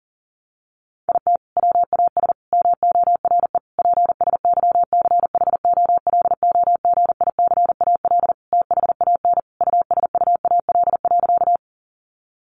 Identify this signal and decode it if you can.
Morse code, heard as a keyed tone, about 37 wpm, 700 Hz. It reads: IT WAS MORE PSYCHOLOGICAL THAN USUAL.
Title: Sentences from Top 1400 Words 40wpm